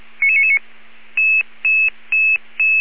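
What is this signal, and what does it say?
Five-tone selective call (ZVEI), heard as quick runs of five tones, tones of about 70 ms. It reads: 99999
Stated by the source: ZVEI five-tone selective call 77777 received by SDR.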